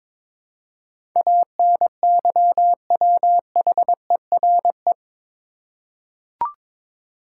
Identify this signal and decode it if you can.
Morse code, heard as a keyed tone, about 22 wpm, 700 Hz. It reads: ANYWHERE
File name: practice / 22wpm / Eight Letter Words - 22wpm